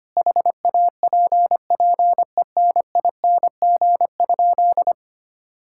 Morse code, heard as a keyed tone, about 25 wpm, 700 Hz. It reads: HAPPENING?
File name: practice / 25wpm / Sentences with 2-Word Phrases - 25wpm